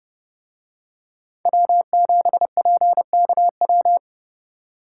Morse code, heard as a keyed tone, about 27 wpm, 700 Hz. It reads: W7PKW